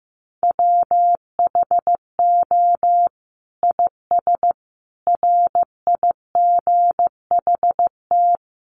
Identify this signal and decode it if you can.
Morse code, heard as a keyed tone, about 15 wpm, 700 Hz. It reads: WHO IS RIGHT
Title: Sentences from Top 900 Words 15wpm